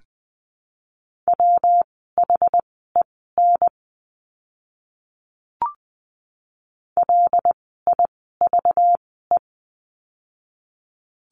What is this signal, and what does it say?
Morse code, heard as a keyed tone, about 20 wpm, spaced as for 14 wpm, 700 Hz. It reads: WHEN LIVE
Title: 192 Lesson 31f - Letter v - Words 20wpm